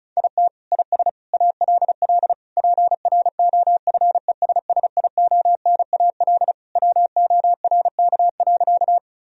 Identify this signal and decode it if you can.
Morse code, heard as a keyed tone, about 34 wpm, 700 Hz. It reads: IT IS ALL PROFESSIONAL WORK.
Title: Sentences from Top 1000 Words - Review Part 4 - 35wpm